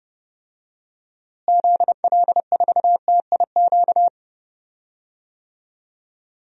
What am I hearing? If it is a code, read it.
Morse code, heard as a keyed tone, about 30 wpm, 700 Hz. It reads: ZL4TIQ
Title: DXCC-100 Call Signs - 6 Characters - Repeated 3x - 30wpm